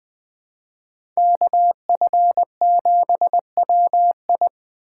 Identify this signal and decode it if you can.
Morse code, heard as a keyed tone, about 20 wpm, 700 Hz. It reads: KF7WI
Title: 2023 SOTA Chaser Call Signs - 20wpm